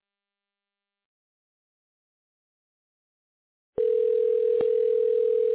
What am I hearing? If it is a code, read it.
Ringback tone (North America), 440 + 480 Hz.